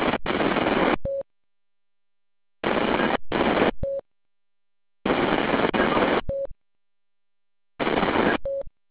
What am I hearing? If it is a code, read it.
There is no code.